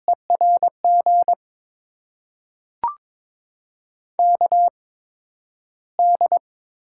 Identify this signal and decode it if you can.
Morse code, heard as a keyed tone, about 23 wpm, spaced as for 12 wpm, 700 Hz. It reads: ERG KD
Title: USA Call Signs - 6 Characters - 8x Character Spacing - 22wpm